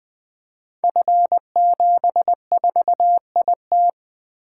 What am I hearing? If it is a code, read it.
Morse code, heard as a keyed tone, about 20 wpm, 700 Hz. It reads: F74IT
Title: Code Groups - 5 Letter:Numbers - 20wpm